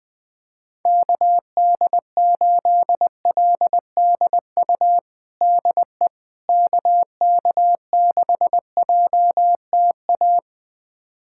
Morse code, heard as a keyed tone, about 20 wpm, 700 Hz. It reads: KD8LDU DE KK6JTA
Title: USA Callsign DE Callsign - 6 Characters - 20wpm